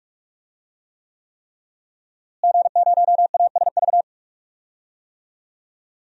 Morse code, heard as a keyed tone, about 44 wpm, 700 Hz. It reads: G0ASV
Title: DXCC-100 Call Signs - 5 Characters - Repeated 3x - 45wpm